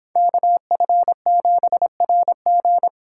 Morse code, heard as a keyed tone, about 25 wpm, 700 Hz. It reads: KF7RG